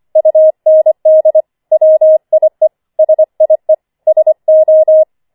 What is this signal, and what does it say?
Morse code, heard as a keyed tone, about 25 wpm, 630 Hz. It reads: UND WIE SIE SO